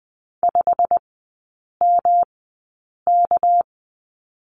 Morse code, heard as a keyed tone, about 20 wpm, spaced as for 8 wpm, 700 Hz. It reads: 5MK
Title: 198 Lesson 32e - Letter k - Three Characters 20wpm